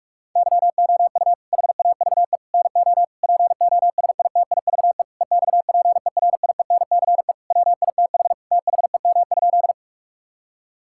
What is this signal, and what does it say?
Morse code, heard as a keyed tone, about 41 wpm, 700 Hz. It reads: YOU HAVE NO POSITIVE EXPERIENCE WITH THEM?
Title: Sentences from Top 1000 Words - Review Part 5 - 45wpm